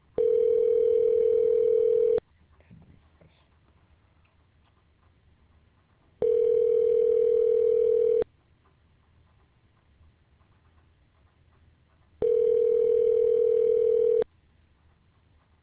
Ringback tone (North America), 440 + 480 Hz.